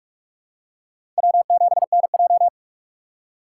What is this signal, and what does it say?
Morse code, heard as a keyed tone, about 46 wpm, 700 Hz. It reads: W7NJ
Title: USA Call Signs - 2x Character Spacing - 45wpm